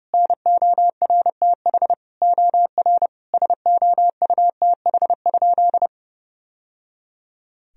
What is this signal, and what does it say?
Morse code, heard as a keyed tone, about 32 wpm, 700 Hz. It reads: NORTH OR SOUTH?
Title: Sentences from Top 1000 Words - Review Part 2 - 30wpm